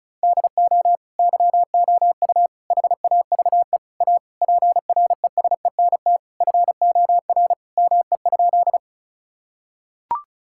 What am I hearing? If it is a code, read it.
Morse code, heard as a keyed tone, about 36 wpm, 700 Hz. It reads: DO YOU HAVE A PRESENT FOR ME?
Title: Sentences from Top 1000 Words - Review Part 5 - 35wpm